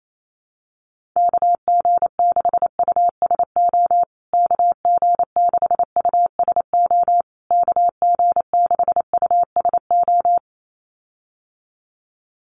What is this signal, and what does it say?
Morse code, heard as a keyed tone, about 28 wpm, 700 Hz. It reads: KG6USO KG6USO KG6USO
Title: USA Call Signs Repeated 3x 28wpm